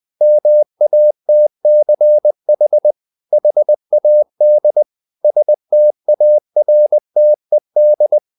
Morse code, heard as a keyed tone, about 20 wpm, 600 Hz. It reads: MATCH HAD STARTED